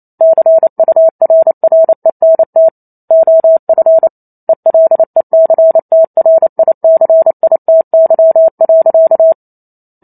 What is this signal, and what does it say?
Morse code, heard as a keyed tone, about 28 wpm, 650 Hz. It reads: CURRENT OF ELECTRICITY.
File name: Chapter 2 - 28 WPM